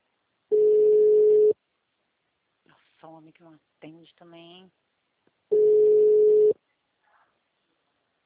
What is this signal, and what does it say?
Ringback tone (Europe, 425 Hz), with speech.